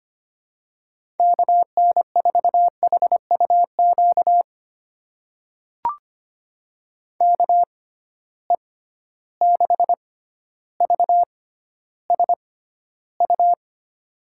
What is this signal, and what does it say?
Morse code, heard as a keyed tone, about 25 wpm, spaced as for 13 wpm, 700 Hz. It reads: KN4HUQ KE6VSU
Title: USA Call Signs - 6 Characters - 6x Character Spacing - 25wpm